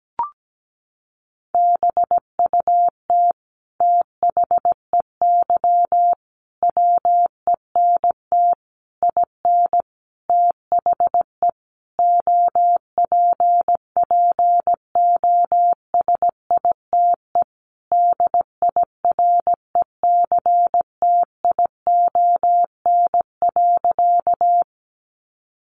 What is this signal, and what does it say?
Morse code, heard as a keyed tone, about 17 wpm, 700 Hz. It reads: BUT THEY WENT IN THE OPPOSITE DIRECTION.
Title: Sentences from Top 1300 Words 17wpm